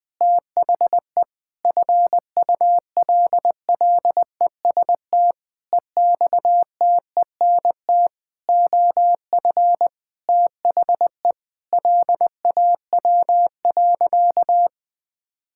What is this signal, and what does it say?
Morse code, heard as a keyed tone, about 20 wpm, 700 Hz. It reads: THE FULLEST EXTENT OF THE LAW.
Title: Sentences with 7-Word Phrases - 20wpm